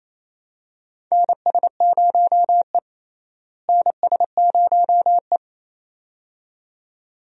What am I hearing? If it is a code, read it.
Morse code, heard as a keyed tone, about 28 wpm, 700 Hz. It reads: NS0E NS0E
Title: USA Call Signs - 4 Characters - Repeated 2x - 28wpm